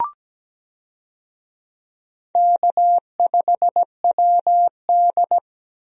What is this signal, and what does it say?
Morse code, heard as a keyed tone, about 17 wpm, 700 Hz. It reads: K5WD
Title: USA Call Signs - 4 Characters - Repeated 3x - 17wpm